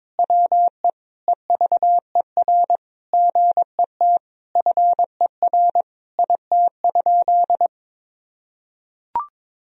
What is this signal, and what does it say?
Morse code, heard as a keyed tone, about 22 wpm, 700 Hz. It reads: WE EVER GET FER IT?
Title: Sentences from Top 400 Words and Abbreviations 22wpm